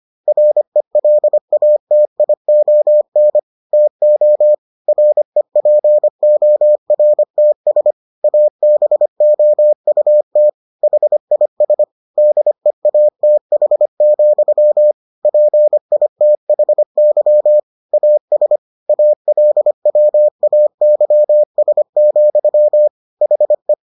Morse code, heard as a keyed tone, about 25 wpm, 600 Hz. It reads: RELATION TO REPORTS ABOUT HIS DEATH, PITHY AS ALWAYS, HE